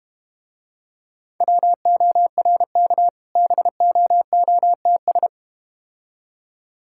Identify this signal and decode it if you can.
Morse code, heard as a keyed tone, about 32 wpm, 700 Hz. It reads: WORK BOOTS